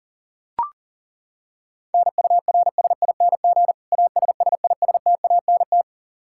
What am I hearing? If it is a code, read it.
Morse code, heard as a keyed tone, about 40 wpm, 700 Hz. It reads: NURSING ASSISTANT
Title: Practice QSO - Exchange 3 - More Info - 40wpm